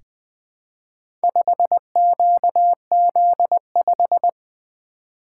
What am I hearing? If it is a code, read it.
Morse code, heard as a keyed tone, about 20 wpm, 700 Hz. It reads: 5QZ5
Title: Four Letter-Number Combinations - 20wpm